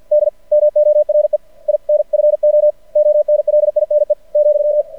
Morse code, heard as a keyed tone, about 53 wpm, 600 Hz. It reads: D MORE INFO ONLINE ,
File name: archived / CRSnet